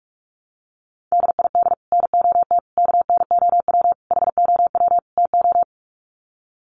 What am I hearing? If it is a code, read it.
Morse code, heard as a keyed tone, about 44 wpm, 700 Hz. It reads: DID NOT KNOW HOW TO
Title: Sets of 5 Words 45wpm